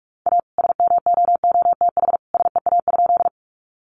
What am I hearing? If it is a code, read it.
Morse code, heard as a keyed tone, about 44 wpm, 700 Hz. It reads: A SMOOTH SEA?
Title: Sentences from Top 1400 Words 45wpm